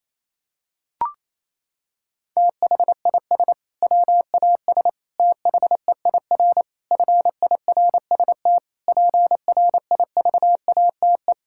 Morse code, heard as a keyed tone, about 28 wpm, 700 Hz. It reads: THIS WAS THEIR FIRST PRIVATE
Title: Sentences from Top 1600 Words 28wpm